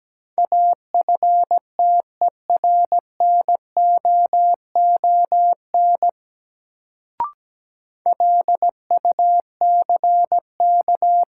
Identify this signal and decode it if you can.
Morse code, heard as a keyed tone, about 17 wpm, 700 Hz. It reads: AFTERNOON LUCK
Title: Sentences from Top 1500 Words - New Word Drill Alt - 17wpm